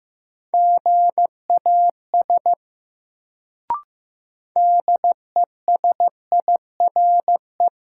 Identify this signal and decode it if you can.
Morse code, heard as a keyed tone, about 15 wpm, 700 Hz. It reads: GAS DESIRE
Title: Sentences from Top 1100 Words - New Word Drill Alt - 15wpm